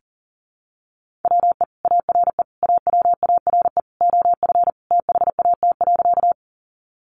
Morse code, heard as a keyed tone, about 40 wpm, 700 Hz. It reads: WE ARE AWARE OF THAT.
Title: Sentences from Top 1200 Words 40wpm